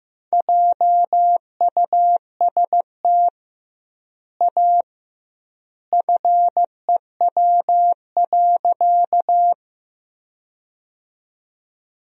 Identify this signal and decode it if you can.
Morse code, heard as a keyed tone, about 15 wpm, 700 Hz. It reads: JUST A FEW.
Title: 2x Word-Spacing - Sentences from Top 300 Words - 15wpm